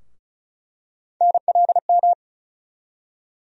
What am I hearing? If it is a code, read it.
Morse code, heard as a keyed tone, about 36 wpm, 700 Hz. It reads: NLM
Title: USA Call Sign Postfixes - 3 Letters 35wpm